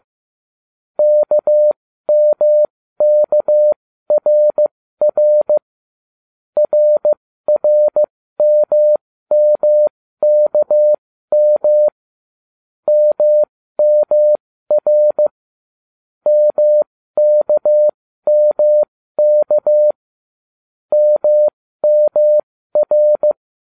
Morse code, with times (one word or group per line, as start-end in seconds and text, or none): 1.0-5.6 KMKRR
6.6-11.9 RRMMKM
12.9-15.3 MMR
16.3-19.9 MKMK
20.9-23.3 MMR